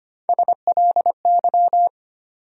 Morse code, heard as a keyed tone, about 25 wpm, 700 Hz. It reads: SLY
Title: USA Call Sign Postfixes - 3 Letters 25wpm